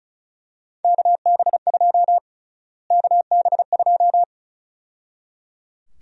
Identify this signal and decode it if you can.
Morse code, heard as a keyed tone, about 34 wpm, 700 Hz. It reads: KB2 KB2
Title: USA Call Sign Prefixes - Repeated 2x - 35wpm